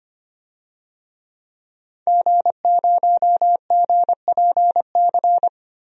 Morse code, heard as a keyed tone, about 25 wpm, 700 Hz. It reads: G0GPC
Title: DXCC-100 Call Signs - 5 Characters - Repeated 3x - 25wpm